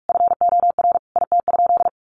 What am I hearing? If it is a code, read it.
Morse code, heard as a keyed tone, about 44 wpm, 700 Hz. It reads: FOR IT?
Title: Sentences from Top 400 Words 45wpm